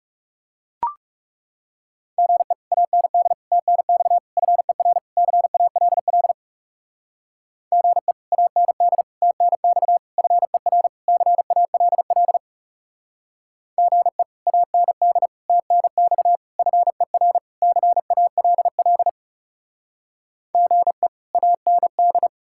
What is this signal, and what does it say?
Morse code, heard as a keyed tone, about 37 wpm, 700 Hz. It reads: GE AND TNX FER CALL GE AND TNX FER CALL GE AND TNX FER CALL GE AND
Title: Speed-Racing Common QSO Phrases Encore - 30wpm